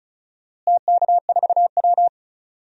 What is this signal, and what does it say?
Morse code, heard as a keyed tone, about 36 wpm, 700 Hz. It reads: TK4W
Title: DXCC-100 Call Signs - 4 Characters - Repeated 3x - 35wpm